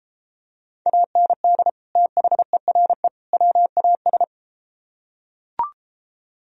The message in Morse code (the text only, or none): AND THERE WAS